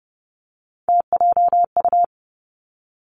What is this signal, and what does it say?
Morse code, heard as a keyed tone, about 30 wpm, 700 Hz. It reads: TJU
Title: Three Challenging Letter-Number Combinations 30wpm